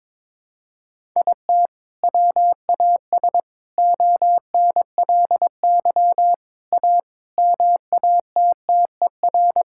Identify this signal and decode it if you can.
Morse code, heard as a keyed tone, about 22 wpm, 700 Hz. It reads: IT WAS ONLY A MATTER
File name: Sets of 9 Words - 22wpm